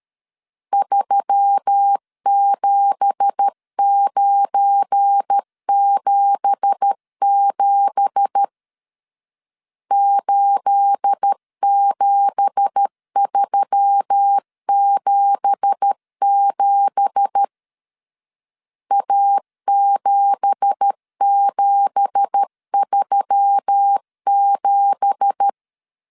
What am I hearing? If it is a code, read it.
Morse code, heard as a keyed tone, about 13 wpm, 800 Hz. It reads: 37977 87377 A7737
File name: LEZIONI / 9 2867 12wpm 2322